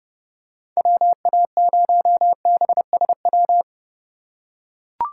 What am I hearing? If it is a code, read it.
Morse code, heard as a keyed tone, about 30 wpm, 700 Hz. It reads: WA0BSW
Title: USA Call Signs - 4x Character Spacing - 30wpm